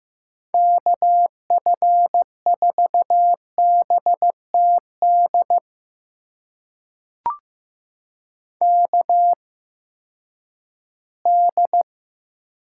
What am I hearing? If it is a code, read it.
Morse code, heard as a keyed tone, about 15 wpm, spaced as for 10 wpm, 700 Hz. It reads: KF4BTD KD
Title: USA Call Signs - 6 Characters - 8x Character Spacing - 15wpm